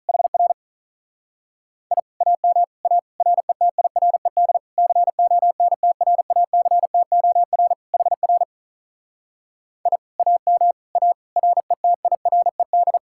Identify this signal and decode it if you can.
Morse code, heard as a keyed tone, about 41 wpm, 700 Hz. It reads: HR I AM A RETIRED CONTRACTOR HR I AM A RETIRED
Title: Speed-Racing Common QSO Phrases - 35wpm